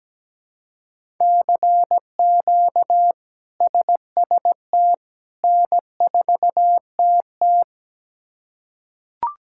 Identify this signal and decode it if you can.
Morse code, heard as a keyed tone, about 17 wpm, 700 Hz. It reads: CQ SST N4TT
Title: K1USN SST QSO Practice - 17wpm